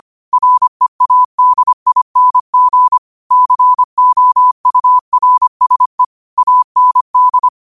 Morse code, heard as a keyed tone, about 25 wpm, 1000 Hz. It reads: READING COURSE AND
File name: 25wpm